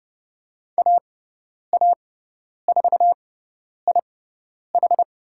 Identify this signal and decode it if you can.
Morse code, heard as a keyed tone, about 30 wpm, spaced as for 10 wpm, 700 Hz. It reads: AA4IH
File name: Lesson 17g - Letter u - Call Signs 30wpm